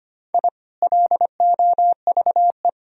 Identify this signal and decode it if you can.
Morse code, heard as a keyed tone, about 25 wpm, 700 Hz. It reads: I LOVE
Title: Sentences from Top 1000 Words - Review Part 4 - 25wpm